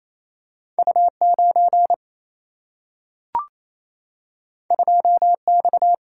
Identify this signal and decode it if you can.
Morse code, heard as a keyed tone, about 28 wpm, 700 Hz. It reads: U9 2X
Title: Single-Letter Single-Number 28wpm